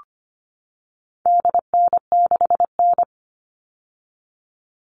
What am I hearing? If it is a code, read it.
Morse code, heard as a keyed tone, about 25 wpm, 700 Hz. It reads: DN6N
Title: All Cut Numbers - 4-Digit 25wpm